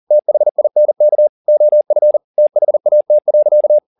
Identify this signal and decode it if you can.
Morse code, heard as a keyed tone, about 40 wpm, 600 Hz. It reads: THINK OF THAT.
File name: the old man and the sea 2 40wpm